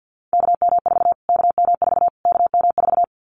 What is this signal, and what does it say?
Morse code, heard as a keyed tone, about 50 wpm, 700 Hz. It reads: KM4 KM4 KM4
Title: Callsign Prefixes for Top 25 DXCC - Repeated 3x - 50wpm